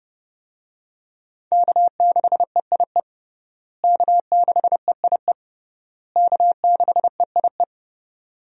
Morse code, heard as a keyed tone, about 30 wpm, 700 Hz. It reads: K6EIE K6EIE K6EIE
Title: USA Call Signs - 5 Characters - Repeated 3x - 30wpm